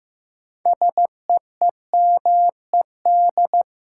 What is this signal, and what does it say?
Morse code, heard as a keyed tone, about 15 wpm, 700 Hz. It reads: SEEMED
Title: Sets of 3 Words - New Word Drill - 15wpm